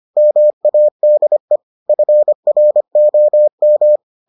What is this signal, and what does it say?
Morse code, heard as a keyed tone, about 25 wpm, 600 Hz. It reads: MADE FROM